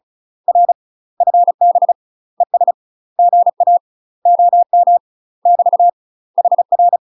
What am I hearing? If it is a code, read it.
Morse code, heard as a keyed tone, about 33 wpm, 700 Hz. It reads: R FB ES GA OM <BT> HR
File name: UK MASTER QSOs TRAINING / 35WPM / UK MASTER QSOs 0012